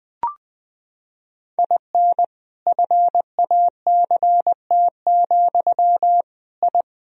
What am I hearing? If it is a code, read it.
Morse code, heard as a keyed tone, about 20 wpm, 700 Hz. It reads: IN FACT, I
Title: Sentences from Top 1000 Words - Review Part 3 - 20wpm